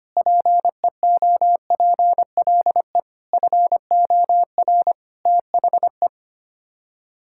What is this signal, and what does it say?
Morse code, heard as a keyed tone, about 25 wpm, 700 Hz. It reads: PEOPLE FOR THE
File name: Sets of 9 Words - 25wpm